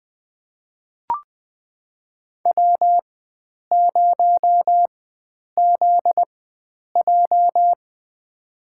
Morse code, heard as a keyed tone, about 20 wpm, spaced as for 9 wpm, 700 Hz. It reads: W0ZJ
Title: USA Call Signs - 4 Characters - 4x Character Spacing - 20wpm